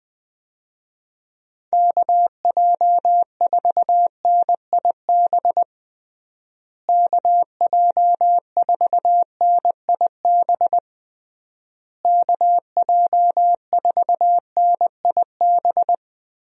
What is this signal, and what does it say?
Morse code, heard as a keyed tone, about 20 wpm, 700 Hz. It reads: KJ4NIB KJ4NIB KJ4NIB
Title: USA Call Signs - 6 Characters - Repeated 3x - 20wpm